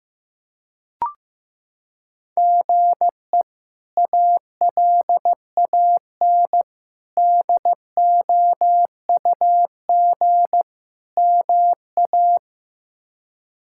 Morse code, heard as a keyed tone, about 15 wpm, 700 Hz. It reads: GE ALAN DOUG MA
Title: K1USN SST QSO Practice - 15wpm